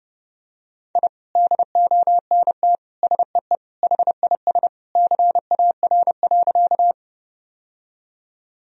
Morse code, heard as a keyed tone, about 30 wpm, 700 Hz. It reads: I DONT SEE HIS CAR.
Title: Sentences from Top 1000 Words - Review Part 1 - 30wpm